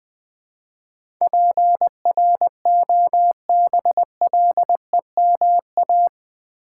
Morse code, heard as a keyed tone, about 20 wpm, 700 Hz. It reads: PROBLEMA
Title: Build A Word - 4 Letters At A Time - 11-Letter Words - 20wpm